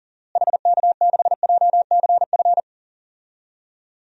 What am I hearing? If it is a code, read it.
Morse code, heard as a keyed tone, about 40 wpm, 700 Hz. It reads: HK6JCF
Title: DXCC-100 Call Signs - 6 Characters - Repeated 2x - 40wpm